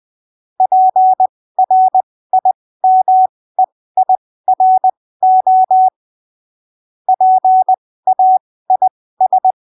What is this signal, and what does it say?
Morse code, heard as a keyed tone, about 20 wpm, spaced as for 15 wpm, 750 Hz. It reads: PRIMEIRO PAIS
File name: arquivos / texto / texto 16 20ppm